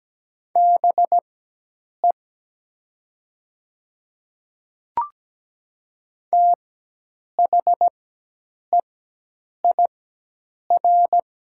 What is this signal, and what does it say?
Morse code, heard as a keyed tone, about 17 wpm, spaced as for 8 wpm, 700 Hz. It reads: BE THEIR
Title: 4x Character Spacing - Top 100 Words - 17wpm